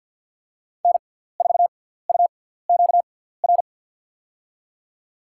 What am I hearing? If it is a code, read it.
Morse code, heard as a keyed tone, about 49 wpm, spaced as for 17 wpm, 700 Hz. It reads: N4UQR
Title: USA Call Signs - 5 Characters - 6x Character Spacing - 50wpm